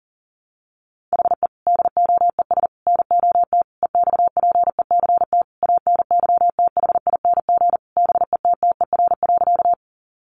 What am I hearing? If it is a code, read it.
Morse code, heard as a keyed tone, about 40 wpm, 700 Hz. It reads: HE DOES NOT EXPECT ANYTHING BETTER.
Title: Sentences from Top 1100 Words 40wpm